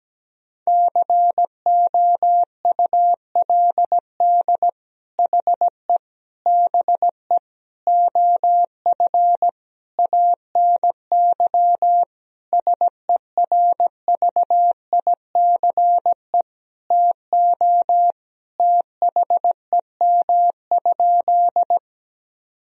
Morse code, heard as a keyed tone, about 17 wpm, 700 Hz. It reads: COULD HE BE OF ANY SERVICE TO THEM?
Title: Sentences from Top 1000 Words - Review Part 4 - 17wpm